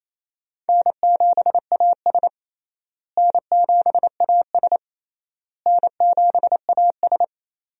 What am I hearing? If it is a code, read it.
Morse code, heard as a keyed tone, about 27 wpm, 700 Hz. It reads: N7AS N7AS N7AS
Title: USA Call Signs - 4 Characters - Repeated 3x - 28wpm